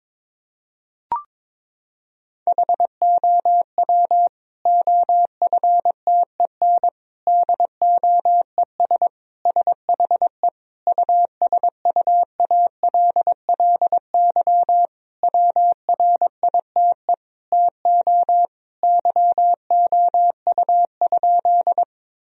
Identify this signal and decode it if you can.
Morse code, heard as a keyed tone, about 22 wpm, 700 Hz. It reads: HOW OFTEN DOES SHE USUALLY WRITE TO YOU?
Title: Sentences from Top 1000 Words - Review Part 3 - 22wpm